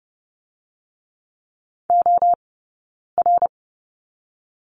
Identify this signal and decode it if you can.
Morse code, heard as a keyed tone, about 30 wpm, 700 Hz. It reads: O R
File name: Lesson 16d - Letter c - Two Characters 30wpm